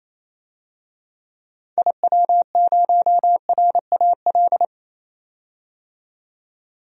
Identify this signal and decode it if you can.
Morse code, heard as a keyed tone, about 28 wpm, 700 Hz. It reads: IW0RAL
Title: DXCC-100 Call Signs - 6 Characters - 28wpm